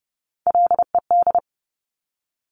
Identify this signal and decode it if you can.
Morse code, heard as a keyed tone, about 30 wpm, 700 Hz. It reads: LED
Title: Top Three Letter Combinations in Words 30wpm